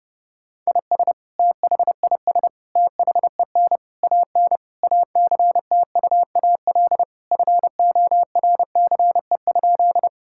Morse code, heard as a keyed tone, about 30 wpm, 700 Hz. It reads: IS THIS THEN AN ACTUAL FORCE?
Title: Sentences from Top 1000 Words - Review Part 5 - 30wpm